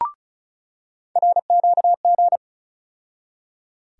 Morse code, heard as a keyed tone, about 34 wpm, 700 Hz. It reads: RQG
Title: USA Call Sign Postfixes - 3 Letters 35wpm